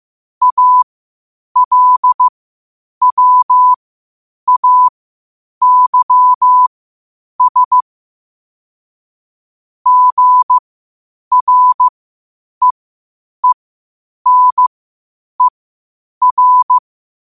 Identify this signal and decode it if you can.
Morse code, heard as a keyed tone, about 15 wpm, spaced as for 8 wpm, 1000 Hz. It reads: ALWAYS GREENER
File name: qotdfiles / 8wpm